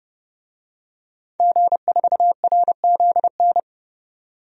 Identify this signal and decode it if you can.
Morse code, heard as a keyed tone, about 30 wpm, 700 Hz. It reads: G4RZN
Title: DXCC-100 Call Signs - 5 Characters - 30wpm